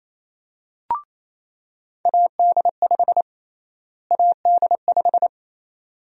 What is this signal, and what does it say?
Morse code, heard as a keyed tone, about 28 wpm, 700 Hz. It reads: AD5 AD5
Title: USA Call Sign Prefixes - Repeated 2x - 28wpm